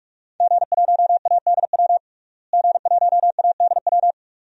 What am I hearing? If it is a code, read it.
Morse code, heard as a keyed tone, about 47 wpm, 700 Hz. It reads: G1ADW G1ADW
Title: DXCC-100 Call Signs - 5 Characters - Repeated 2x - 45wpm